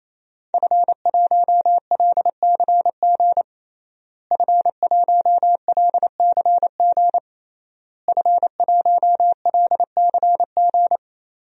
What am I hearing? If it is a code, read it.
Morse code, heard as a keyed tone, about 27 wpm, 700 Hz. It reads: F1LCG F1LCG F1LCG
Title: DXCC-100 Call Signs - 5 Characters - Repeated 3x - 28wpm